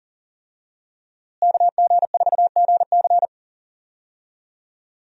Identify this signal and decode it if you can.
Morse code, heard as a keyed tone, about 39 wpm, 700 Hz. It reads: KG4GC
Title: USA Call Signs - 5 Characters - Repeated 3x - 40wpm